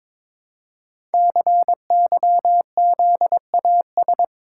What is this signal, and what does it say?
Morse code, heard as a keyed tone, about 22 wpm, 700 Hz. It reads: CYZAS
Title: Code Groups - 5 Letters - 22wpm